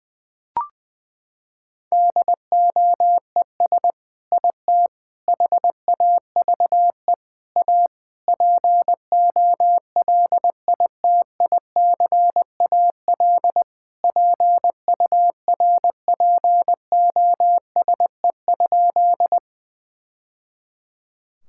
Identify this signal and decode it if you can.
Morse code, heard as a keyed tone, about 20 wpm, 700 Hz. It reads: DOES IT HAVE A POLITICAL PURPOSE?
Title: Sentences from Top 1000 Words - Review Part 5 - 20wpm